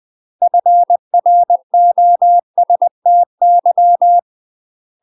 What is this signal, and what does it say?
Morse code, heard as a keyed tone, about 20 wpm, 700 Hz. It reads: FROSTY
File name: UK MASTER QSOs TRAINING/20WPM